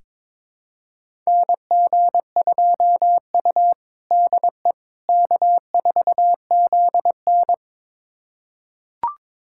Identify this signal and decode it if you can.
Morse code, heard as a keyed tone, about 22 wpm, 700 Hz. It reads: NG2U DE K4ZN
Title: USA Callsign DE Callsign - 4 Characters - 22wpm